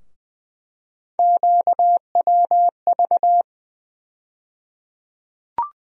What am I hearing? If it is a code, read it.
Morse code, heard as a keyed tone, about 20 wpm, 700 Hz. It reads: QWV
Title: USA Call Sign Postfixes - 3 Letters 20wpm